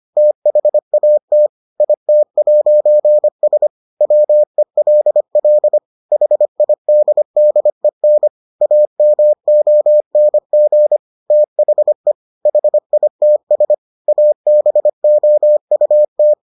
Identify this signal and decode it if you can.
Morse code, heard as a keyed tone, about 25 wpm, 600 Hz. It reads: THAT IT'S WELL HIDDEN AMONG THE HITS ABOUT